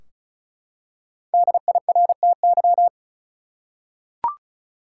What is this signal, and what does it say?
Morse code, heard as a keyed tone, about 34 wpm, 700 Hz. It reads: DIRTY
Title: Five Letter Words - 35wpm